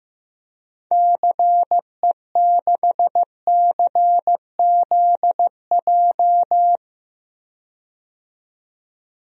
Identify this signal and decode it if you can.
Morse code, heard as a keyed tone, about 15 wpm, 700 Hz. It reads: CE6CZJ
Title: DXCC-100 Call Signs - 6 Characters - 15wpm